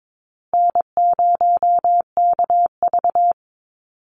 Morse code, heard as a keyed tone, about 22 wpm, 700 Hz. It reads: N0KV
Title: Call Signs Repeated 3x - 22wpm